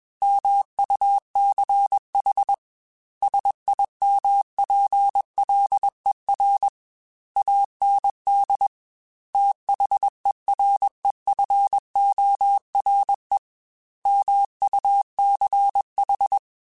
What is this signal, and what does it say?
Morse code, heard as a keyed tone, about 21 wpm, 800 Hz. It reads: MUCH SIMPLER AND THEREFORE MUCH
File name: Technical Subjects / Technical Subjects-mp3s / AND Gate @20